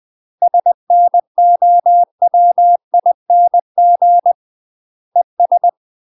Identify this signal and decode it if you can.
Morse code, heard as a keyed tone, about 20 wpm, 700 Hz. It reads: SNOWING ES
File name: UK MASTER QSOs TRAINING / 20WPM / UK MASTER QSOs 0045